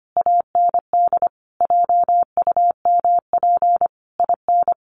Morse code, heard as a keyed tone, about 25 wpm, 700 Hz. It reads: AND JUMP IN
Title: Sentences from Top 1300 Words 25wpm